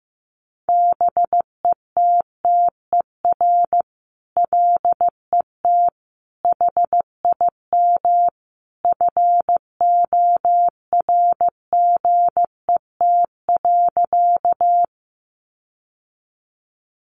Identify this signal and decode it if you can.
Morse code, heard as a keyed tone, about 15 wpm, 700 Hz. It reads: BETTER LET HIM FORGET.
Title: Sentences from Top 1200 Words 15wpm